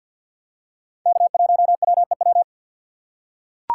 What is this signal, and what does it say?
Morse code, heard as a keyed tone, about 52 wpm, 700 Hz. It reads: K1WEW
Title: USA Call Signs - 5 Characters - 2x Character Spacing - 50wpm